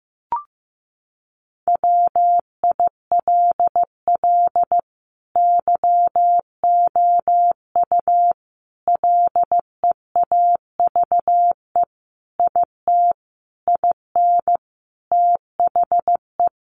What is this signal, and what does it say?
Morse code, heard as a keyed tone, about 15 wpm, 700 Hz. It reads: WILL YOU LEAVE IT IN THE
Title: Sentences from Top 1300 Words 15wpm